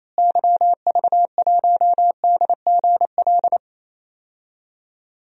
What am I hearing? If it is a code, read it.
Morse code, heard as a keyed tone, about 28 wpm, 700 Hz. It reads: YV1DGL
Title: DXCC-100 Call Signs - 6 Characters - Repeated 3x - 28wpm